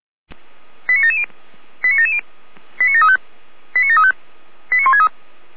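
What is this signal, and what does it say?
Five-tone selective call (ZVEI), heard as quick runs of five tones, tones of about 80 ms. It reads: 89770 89770 89735 89735 89163
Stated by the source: ZVEI five-tone call 89720 received by SDR.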